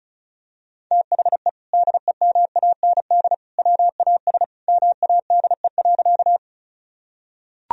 Morse code, heard as a keyed tone, about 34 wpm, 700 Hz. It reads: THE DEMAND WAS MADE.
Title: Sentences from Top 1000 Words - Review Part 4 - 35wpm